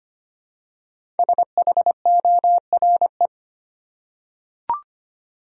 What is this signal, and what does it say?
Morse code, heard as a keyed tone, about 25 wpm, 700 Hz. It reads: SHORE
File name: 5 Letter Words - First 3 Letters Given - 25wpm